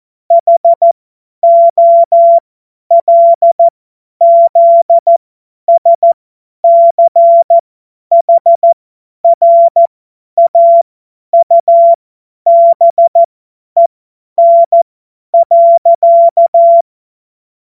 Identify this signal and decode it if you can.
Morse code, heard as a keyed tone, about 14 wpm, spaced as for 10 wpm, 680 Hz. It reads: HOLZSCHRAUBEN.
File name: Morsekurs / Witze / Witz 02. Dezember 2025 50 F70